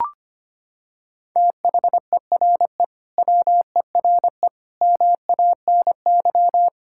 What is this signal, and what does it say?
Morse code, heard as a keyed tone, about 25 wpm, 700 Hz. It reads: THERE WERE MANY
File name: practice / 25wpm / Sentences with 3-Word Phrases - 25wpm